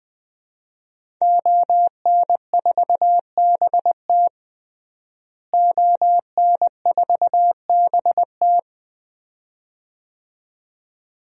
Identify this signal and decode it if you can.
Morse code, heard as a keyed tone, about 20 wpm, 700 Hz. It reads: ON4BT ON4BT
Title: DXCC-100 Call Signs - 5 Characters - Repeated 2x - 20wpm